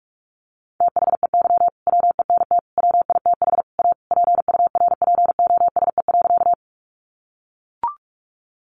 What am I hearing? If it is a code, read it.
Morse code, heard as a keyed tone, about 44 wpm, 700 Hz. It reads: THEY WENT WITH A PURPOSE.
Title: Sentences from Top 800 Words 45wpm